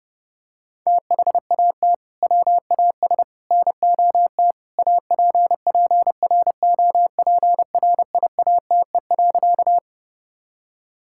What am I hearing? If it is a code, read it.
Morse code, heard as a keyed tone, about 30 wpm, 700 Hz. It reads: THAT WAS NOT APPROPRIATE.
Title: Sentences from Top 1000 Words - Review Part 5 - 30wpm